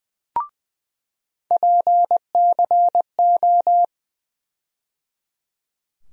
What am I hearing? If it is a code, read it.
Morse code, heard as a keyed tone, about 20 wpm, 700 Hz. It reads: PCO
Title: USA Call Sign Postfixes - 3 Letters 20wpm